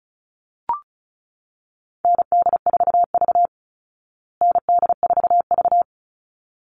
Morse code, heard as a keyed tone, about 34 wpm, 700 Hz. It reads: ND4V ND4V
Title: Call Signs Repeated 3x - 35wpm